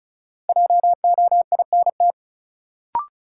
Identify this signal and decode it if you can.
Morse code, heard as a keyed tone, about 36 wpm, 700 Hz. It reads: JOINT
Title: Sentences from Top 1300 Words - New Word Drill Alt - 35wpm